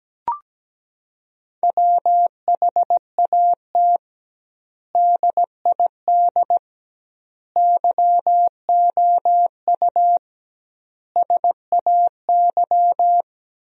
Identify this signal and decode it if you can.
Morse code, heard as a keyed tone, about 17 wpm, 700 Hz. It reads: WHAT DID YOU SAY
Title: 2x Word-Spacing - Sentences from Top 200 Words - 17wpm